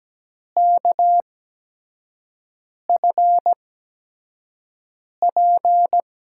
Morse code, heard as a keyed tone, about 17 wpm, spaced as for 5 wpm, 700 Hz. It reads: KFP
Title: USA Call Signs - 5 Characters - 8x Character Spacing - 17wpm